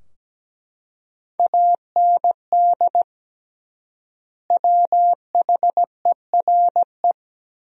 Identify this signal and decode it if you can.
Morse code, heard as a keyed tone, about 17 wpm, 700 Hz. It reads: AND WHERE
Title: 3x Word-Spacing - Sentences from Top 200 Words - 17wpm